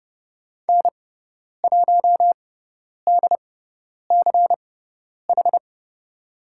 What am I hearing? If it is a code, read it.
Morse code, heard as a keyed tone, about 30 wpm, spaced as for 10 wpm, 700 Hz. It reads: N1DCH
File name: Lesson 18g - Letter m - Call Signs 30wpm